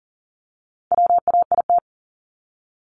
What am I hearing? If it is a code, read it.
Morse code, heard as a keyed tone, about 40 wpm, 700 Hz. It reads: WAIT
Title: Top 500 Words 40wpm